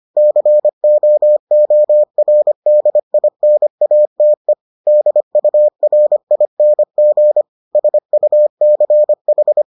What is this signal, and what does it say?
Morse code, heard as a keyed tone, about 25 wpm, 600 Hz. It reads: COORDINATE DURING SUCH